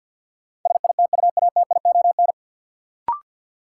Morse code, heard as a keyed tone, about 47 wpm, 700 Hz. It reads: SITUATION
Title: Sentences from Top 1200 Words - New Word Drill Alt - 50wpm